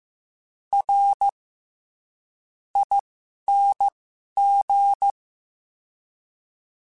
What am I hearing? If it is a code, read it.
Morse code, heard as a keyed tone, about 15 wpm, spaced as for 11 wpm, 800 Hz. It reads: R ING
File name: Special INGs-mp3s / Special INGs-mp3s / SlowToFaster ing @10